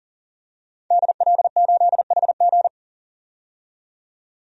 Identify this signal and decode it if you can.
Morse code, heard as a keyed tone, about 40 wpm, 700 Hz. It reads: DL8HG
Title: DXCC-100 Call Signs - 5 Characters - 40wpm